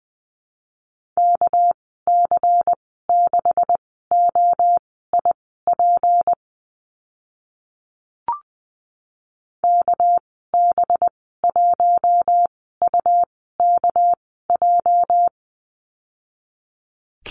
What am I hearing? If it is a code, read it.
Morse code, heard as a keyed tone, about 20 wpm, spaced as for 14 wpm, 700 Hz. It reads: KC6OIP KB1UKJ
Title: 213 Lesson 34g - Number 8 - Call Signs 20wpm